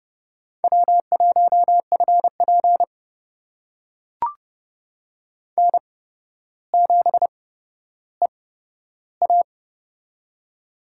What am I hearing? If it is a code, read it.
Morse code, heard as a keyed tone, about 30 wpm, spaced as for 13 wpm, 700 Hz. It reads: W1FP N7EA
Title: USA Call Signs - 4 Characters - 8x Character Spacing - 30wpm